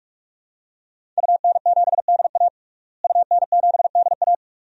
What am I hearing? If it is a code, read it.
Morse code, heard as a keyed tone, about 44 wpm, 700 Hz. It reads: UN7DA UN7DA
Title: DXCC-100 Call Signs - 5 Characters - Repeated 2x - 45wpm